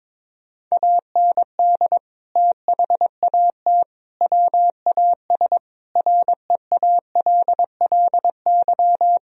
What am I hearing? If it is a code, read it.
Morse code, heard as a keyed tone, about 22 wpm, 700 Hz. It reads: AND THAT WAS REALLY
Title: Sentences from Top 1000 Words - Review Part 3 - 22wpm